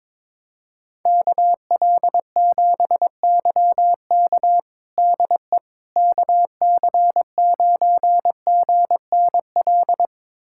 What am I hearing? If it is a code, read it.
Morse code, heard as a keyed tone, about 22 wpm, 700 Hz. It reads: KL7YK DE KC9GNL
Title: USA Callsign DE Callsign 22wpm